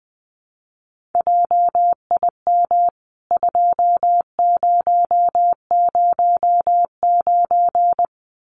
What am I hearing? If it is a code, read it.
Morse code, heard as a keyed tone, about 20 wpm, 700 Hz. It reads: JIM 2009
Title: CWT Exchange 20wpm